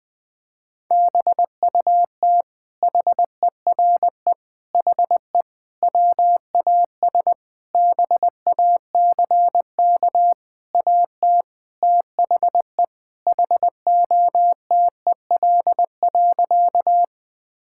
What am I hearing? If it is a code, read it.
Morse code, heard as a keyed tone, about 20 wpm, 700 Hz. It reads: BUT HERE HE WAS BACK AT THE HOTEL.
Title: Sentences from Top 1600 Words 20wpm